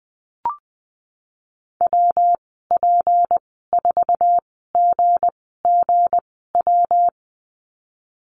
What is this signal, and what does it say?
Morse code, heard as a keyed tone, about 20 wpm, spaced as for 15 wpm, 700 Hz. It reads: WP4GGW
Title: 161 Lesson 26g - Letter g - Call Signs 20wpm